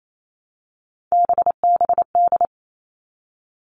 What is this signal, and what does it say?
Morse code, heard as a keyed tone, about 28 wpm, 700 Hz. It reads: BBD
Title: All Cut Numbers - 3-Digit 28wpm